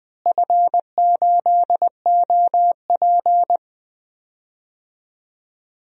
Morse code, heard as a keyed tone, about 21 wpm, 700 Hz. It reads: F8OP
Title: DXCC-100 Call Signs - 4 Characters - Repeated 3x - 20wpm